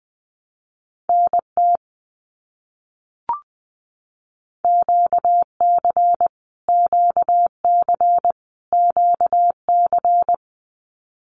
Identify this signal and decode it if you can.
Morse code, heard as a keyed tone, about 20 wpm, 700 Hz. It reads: NT QC QC QC
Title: Canadian Provinces and Territories - Repeated 3x - 20wpm